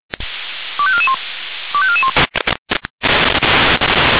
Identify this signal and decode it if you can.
Five-tone selective call (ZVEI), heard as quick runs of five tones, tones of about 70 ms. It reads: 25441 25441